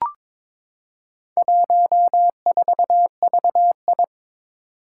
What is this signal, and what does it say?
Morse code, heard as a keyed tone, about 22 wpm, 700 Hz. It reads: 14VI
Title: Four Challenging Letter-Number Combinations 22wpm